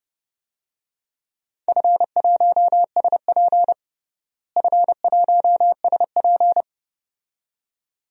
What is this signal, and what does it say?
Morse code, heard as a keyed tone, about 29 wpm, 700 Hz. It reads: F1SP F1SP
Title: DXCC-100 Call Signs - 4 Characters - Repeated 2x - 30wpm